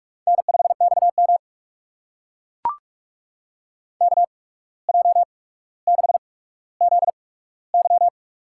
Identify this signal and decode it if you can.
Morse code, heard as a keyed tone, about 44 wpm, spaced as for 18 wpm, 700 Hz. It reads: N5XM KJ6ZY
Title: USA Call Signs - 8x Character Spacing - 45wpm